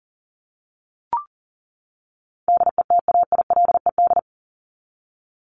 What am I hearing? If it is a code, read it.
Morse code, heard as a keyed tone, about 40 wpm, 700 Hz. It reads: DETAILED
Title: Top 1251-1500 Words 40wpm